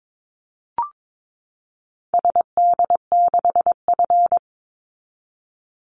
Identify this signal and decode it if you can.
Morse code, heard as a keyed tone, about 22 wpm, 700 Hz. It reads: SD6F
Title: Call Signs Repeated 3x - 22wpm